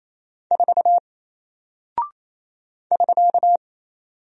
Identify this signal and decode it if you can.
Morse code, heard as a keyed tone, about 27 wpm, 700 Hz. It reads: 4 <SK>
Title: Single Character (Letter:Number:Punctuation:Prosign) 28wpm